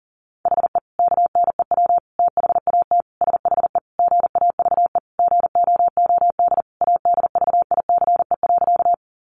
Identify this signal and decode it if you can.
Morse code, heard as a keyed tone, about 40 wpm, 700 Hz. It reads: HE KNEW THAT SHE GAVE GOOD ADVICE.
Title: Sentences from Top 1100 Words 40wpm